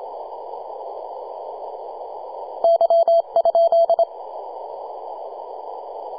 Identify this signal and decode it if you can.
Morse code, heard as a keyed tone, about 25 wpm, 670 Hz. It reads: Y?
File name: posts / 2015-sept-26-2040z-v73a-21-025-cw